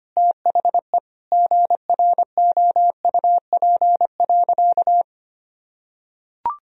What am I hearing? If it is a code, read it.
Morse code, heard as a keyed tone, about 25 wpm, 700 Hz. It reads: THE GROUP.
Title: Sentences from Top 1000 Words - Review Part 1 - 25wpm